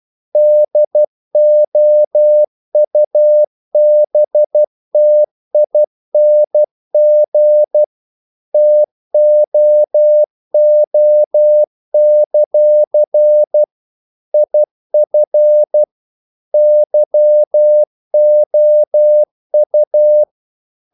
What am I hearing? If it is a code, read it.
Morse code, heard as a keyed tone, about 12 wpm, 600 Hz. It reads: DOUBTING TOO; IF YOU
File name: CW / Practice / If12